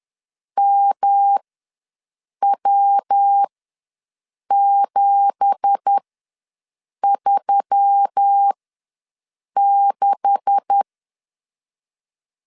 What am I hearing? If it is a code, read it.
Morse code, heard as a keyed tone, about 10 wpm, spaced as for 6 wpm, 800 Hz. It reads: MW736
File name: LEZIONI / 9 2867 ws 10wpm 4215